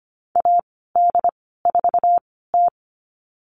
Morse code, heard as a keyed tone, about 25 wpm, spaced as for 16 wpm, 700 Hz. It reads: AD4T
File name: Lesson 15g - Number 5 - Call Signs 25wpm